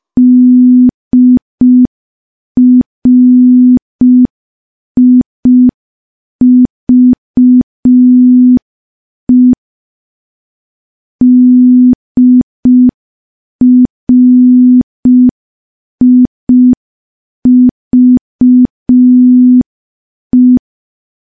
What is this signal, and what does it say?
Morse code, heard as a keyed tone, about 5 wpm, 250 Hz. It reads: DRIVE DRIVE
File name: morse / morse-training-05WPM